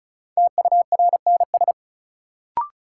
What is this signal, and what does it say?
Morse code, heard as a keyed tone, about 36 wpm, 700 Hz. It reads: TURNS
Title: Sets of 5 Words Encore - New Word Drill - 35wpm